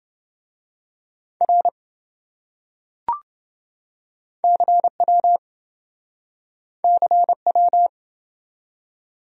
Morse code, heard as a keyed tone, about 25 wpm, 700 Hz. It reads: R CW CW
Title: Speed-Racing Abbreviations - 20wpm